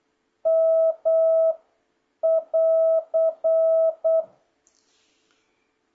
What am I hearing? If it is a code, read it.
Morse code, heard as a keyed tone, about 8 wpm, 650 Hz. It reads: M <AR>